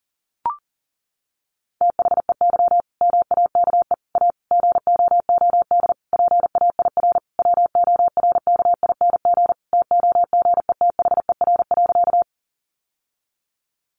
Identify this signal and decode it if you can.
Morse code, heard as a keyed tone, about 40 wpm, 700 Hz. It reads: THEY MAKE A GOOD PAIR WORKING TOGETHER.
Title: Sentences from Top 1200 Words 40wpm